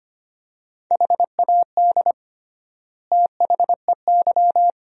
Morse code, heard as a keyed tone, about 25 wpm, 700 Hz. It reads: HAD THEY
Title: 3x Word-Spacing - Sentences from Top 400 Words - 25wpm